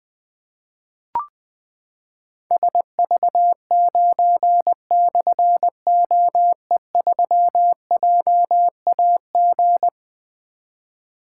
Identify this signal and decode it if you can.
Morse code, heard as a keyed tone, about 21 wpm, 700 Hz. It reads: SV9/OE3JAG
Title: Difficult Call Signs 20wpm